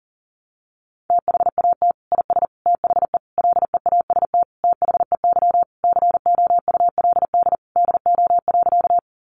Morse code, heard as a keyed tone, about 40 wpm, 700 Hz. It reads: THAT IS THE LEAST THEY COULD DO.
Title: Sentences from Top 700 Words 40wpm